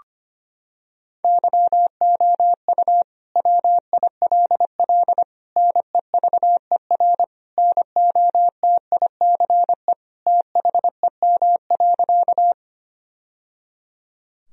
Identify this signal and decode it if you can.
Morse code, heard as a keyed tone, about 25 wpm, 700 Hz. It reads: YOU WILL NEVER NOTICE THEM.
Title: Sentences from Top 1000 Words - Review Part 2 - 25wpm